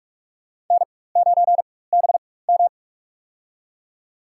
Morse code, heard as a keyed tone, about 44 wpm, spaced as for 21 wpm, 700 Hz. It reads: N9BM
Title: USA Call Signs - 4x Character Spacing - 45wpm